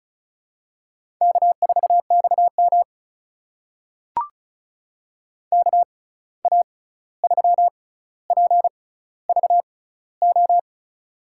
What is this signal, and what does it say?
Morse code, heard as a keyed tone, about 36 wpm, spaced as for 16 wpm, 700 Hz. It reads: K4XM KA3PVO
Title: USA Call Signs - 6x Character Spacing - 35wpm